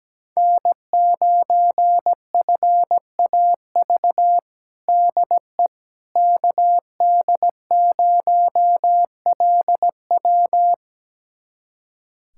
Morse code, heard as a keyed tone, about 17 wpm, 700 Hz. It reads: N9FAV DE KD0LW
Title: USA Callsign DE Callsign - 5 Characters - 17wpm